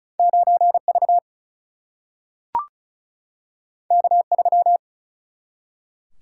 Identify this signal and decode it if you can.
Morse code, heard as a keyed tone, about 36 wpm, 700 Hz. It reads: 9 V K 3
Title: Single-Letter Single-Number 35wpm